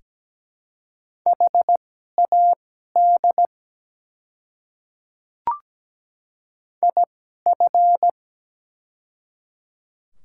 Morse code, heard as a keyed tone, about 17 wpm, spaced as for 12 wpm, 700 Hz. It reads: HAD IF
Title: 2x Character Spacing - Top 300 Words - 17wpm